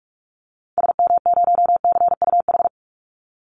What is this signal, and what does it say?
Morse code, heard as a keyed tone, about 44 wpm, 700 Hz. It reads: SM0CUH
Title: Call Signs Repeated 2x - 45wpm